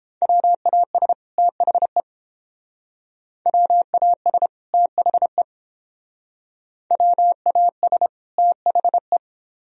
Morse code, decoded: WAS THE WAS THE WAS THE

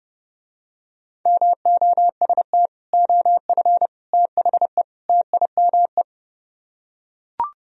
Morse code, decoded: MOST OF THE TIME

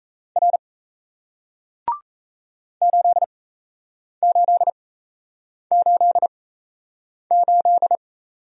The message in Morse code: R 8888

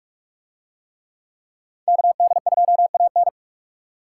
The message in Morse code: KD2AN

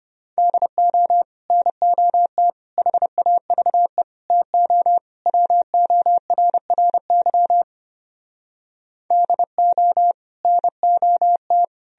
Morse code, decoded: DO NOT HAVE TO WORRY DO NOT

